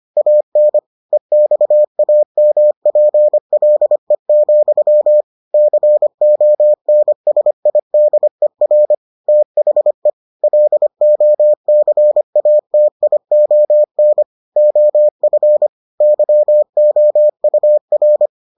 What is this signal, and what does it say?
Morse code, heard as a keyed tone, about 25 wpm, 600 Hz. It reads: AN EXAMPLE, CONSIDER THE LOCATION OF YOUR